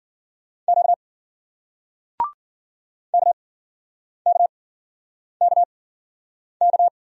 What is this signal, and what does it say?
Morse code, heard as a keyed tone, about 53 wpm, spaced as for 9 wpm, 700 Hz. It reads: Q KKKK